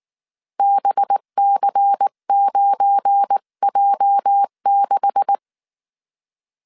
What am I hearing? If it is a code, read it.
Morse code, heard as a keyed tone, about 19 wpm, 800 Hz. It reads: BC9J6
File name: CW / DET / 1 9 1500C S B 18wpm 2551